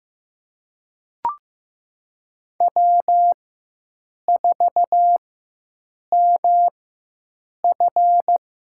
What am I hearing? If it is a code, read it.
Morse code, heard as a keyed tone, about 15 wpm, spaced as for 7 wpm, 700 Hz. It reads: W4MF
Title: USA Call Signs - 4 Characters - 4x Character Spacing - 15wpm